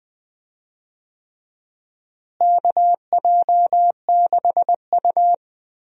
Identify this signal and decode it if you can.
Morse code, heard as a keyed tone, about 21 wpm, 700 Hz. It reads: KJ6U
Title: USA Call Signs - 4 Characters - Repeated 3x - 20wpm